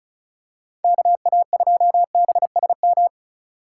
Morse code, heard as a keyed tone, about 34 wpm, 700 Hz. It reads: KA2BSM
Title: USA Call Signs - 3x Character Spacing - 35wpm